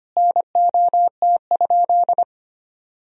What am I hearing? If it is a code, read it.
Morse code, heard as a keyed tone, about 25 wpm, 700 Hz. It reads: NOT?
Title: Sentences from Top 1000 Words - Review Part 1 - 25wpm